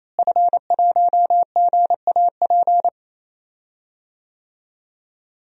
Morse code, heard as a keyed tone, about 28 wpm, 700 Hz. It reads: F1GAP
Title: DXCC-100 Call Signs - 5 Characters - Repeated 3x - 28wpm